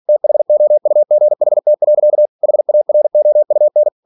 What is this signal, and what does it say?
Morse code, heard as a keyed tone, about 46 wpm, 600 Hz. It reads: THOUGHT. HAROUN
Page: Never 47wpm